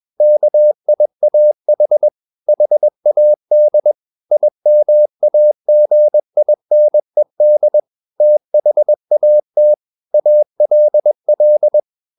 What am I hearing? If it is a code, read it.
Morse code, heard as a keyed tone, about 21 wpm, 600 Hz. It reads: KIAH HAD IMAGINED THAT ALL
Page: CW med Gnister